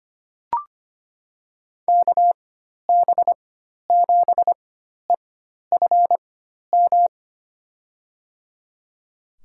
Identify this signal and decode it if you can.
Morse code, heard as a keyed tone, about 25 wpm, spaced as for 12 wpm, 700 Hz. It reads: KB7EFM
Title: USA Call Signs - 6 Characters - 4x Character Spacing - 25wpm